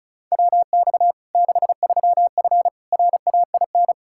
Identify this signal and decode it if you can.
Morse code, heard as a keyed tone, about 36 wpm, 700 Hz. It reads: WX 63F RAIN
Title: Practice QSO - Exchange 2 - Info - 35wpm